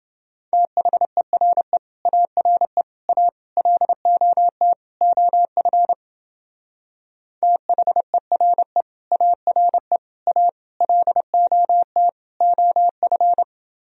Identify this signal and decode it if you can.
Morse code, heard as a keyed tone, about 28 wpm, 700 Hz. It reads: THERE ARE A LOT OF THERE ARE A LOT OF